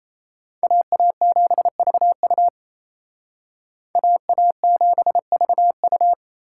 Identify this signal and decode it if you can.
Morse code, heard as a keyed tone, about 35 wpm, 700 Hz. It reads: AA7VU AA7VU